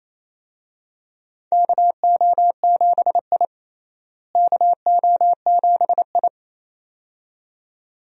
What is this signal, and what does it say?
Morse code, heard as a keyed tone, about 28 wpm, 700 Hz. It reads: KO7I KO7I
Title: USA Call Signs - 4 Characters - Repeated 2x - 28wpm